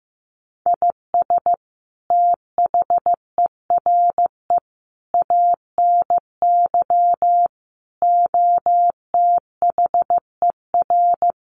Morse code, heard as a keyed tone, about 15 wpm, 700 Hz. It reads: IS THERE ANY OTHER
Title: Sentences from Top 1100 Words 15wpm